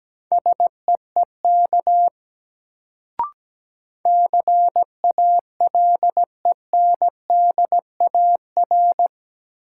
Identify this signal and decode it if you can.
Morse code, heard as a keyed tone, about 17 wpm, 700 Hz. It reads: SEEK CALENDAR
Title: Sentences from Top 1600 Words - New Word Drill Alt - 17wpm